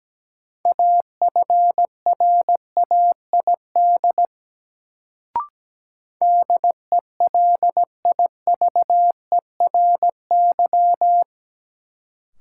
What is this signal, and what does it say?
Morse code, heard as a keyed tone, about 17 wpm, 700 Hz. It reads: AFRAID DELIVERY
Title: Sentences from Top 1500 Words - New Word Drill Alt - 17wpm